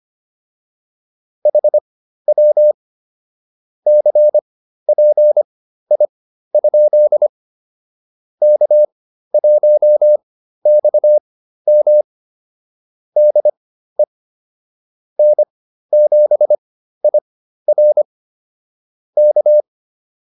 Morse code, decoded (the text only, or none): HW CPI? K1XM DE N7IR K